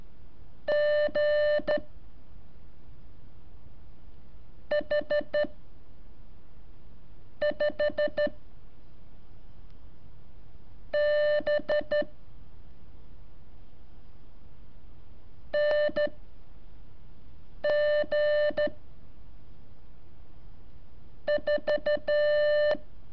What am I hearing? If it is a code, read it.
Morse code, heard as a keyed tone, about 13 wpm, 590 Hz. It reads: G H 5 B N G 4
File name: curso / ejercicio 20